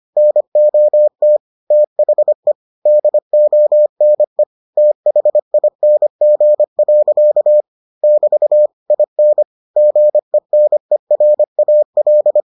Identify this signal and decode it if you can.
Morse code, heard as a keyed tone, about 25 wpm, 600 Hz. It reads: NOT THE DONE THING. <BT> IN GENERAL